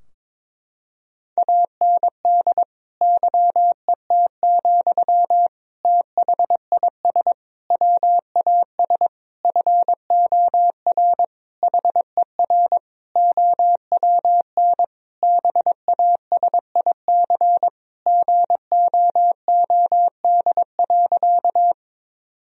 Morse code, heard as a keyed tone, about 22 wpm, 700 Hz. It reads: AND YET, THIS WAS FOR HER OWN BASIC GOOD.
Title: Sentences from Top 1000 Words - Review Part 3 - 22wpm